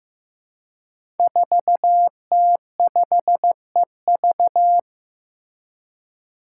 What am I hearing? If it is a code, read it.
Morse code, heard as a keyed tone, about 15 wpm, 700 Hz. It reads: 4T5EV
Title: Five Challenging Letter-Number Combinations 15wpm